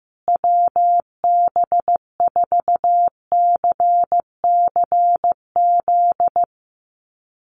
Morse code, heard as a keyed tone, about 15 wpm, 700 Hz. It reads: WB4CCZ
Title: USA Call Signs Repeated 2x 15wpm